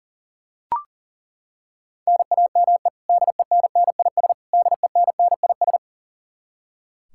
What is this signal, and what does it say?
Morse code, heard as a keyed tone, about 40 wpm, 700 Hz. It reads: NAME DENNIS DENNIS
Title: Practice QSO - Exchange 1 - Beginning - 40wpm